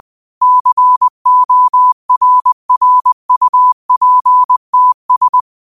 Morse code, heard as a keyed tone, about 20 wpm, 1000 Hz. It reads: CORRUPTS